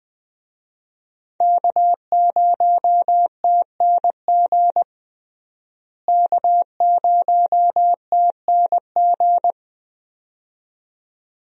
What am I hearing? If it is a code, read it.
Morse code, heard as a keyed tone, about 20 wpm, 700 Hz. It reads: K0TNG K0TNG
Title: USA Call Signs - 5 Characters - Repeated 2x - 20wpm